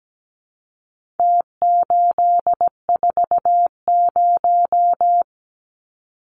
Morse code, heard as a keyed tone, about 17 wpm, 700 Hz. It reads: T840
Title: Numbers - Contest Style - 4-Digit 17wpm